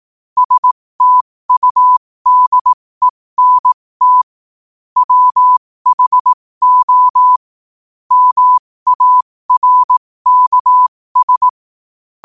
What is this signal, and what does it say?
Morse code, heard as a keyed tone, about 18 wpm, spaced as for 15 wpm, 1000 Hz. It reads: STUDENT WHO MARKS
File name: 15wpm